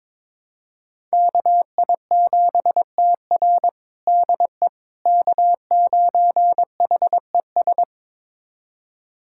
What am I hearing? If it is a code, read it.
Morse code, heard as a keyed tone, about 22 wpm, 700 Hz. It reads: KI7TR DE K9HES
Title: USA Callsign DE Callsign - 5 Characters - 22wpm